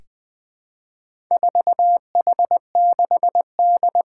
Morse code, heard as a keyed tone, about 20 wpm, 700 Hz. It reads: 4H6D
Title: Four Challenging Letter-Number Combinations 20wpm